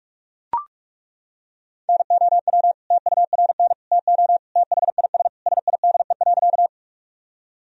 Morse code, heard as a keyed tone, about 41 wpm, 700 Hz. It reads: NOW TURN TO THIS SIDE.
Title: Sentences from Top 1000 Words - Review Part 1 - 45wpm